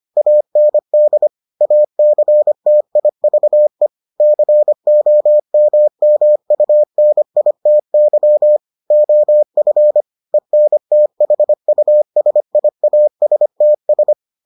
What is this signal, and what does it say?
Morse code, heard as a keyed tone, about 25 wpm, 600 Hz. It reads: AND ACTIVE COMMUNITY OF ENTHUSIASTS